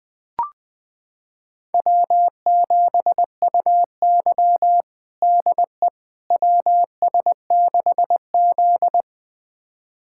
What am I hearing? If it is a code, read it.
Morse code, heard as a keyed tone, about 20 wpm, 700 Hz. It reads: W7UY DE WS6Z
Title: USA Callsign DE Callsign - 4 Characters - 20wpm